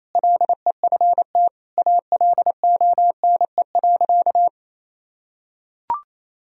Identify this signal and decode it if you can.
Morse code, heard as a keyed tone, about 28 wpm, 700 Hz. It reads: LEFT ALONE.
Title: Sentences from Top 1000 Words - Review Part 4 - 28wpm